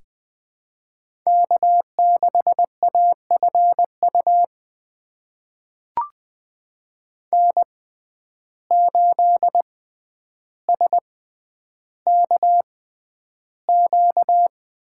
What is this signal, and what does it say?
Morse code, heard as a keyed tone, about 20 wpm, spaced as for 10 wpm, 700 Hz. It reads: K6AFU N8SKQ
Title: USA Call Signs - 5 Characters - 6x Character Spacing - 20wpm